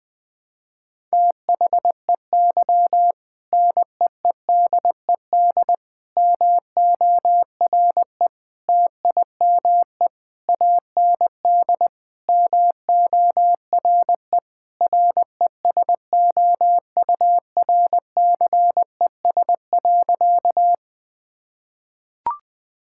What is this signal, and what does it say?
Morse code, heard as a keyed tone, about 20 wpm, 700 Hz. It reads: THEY NEEDED MORE TIME AND MORE RESOURCES.
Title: Sentences with 2-Word Phrases - 20wpm